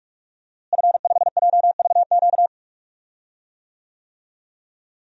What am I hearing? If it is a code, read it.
Morse code, heard as a keyed tone, about 45 wpm, 700 Hz. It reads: F5JVQ
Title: DXCC-100 Call Signs - 5 Characters - Repeated 2x - 45wpm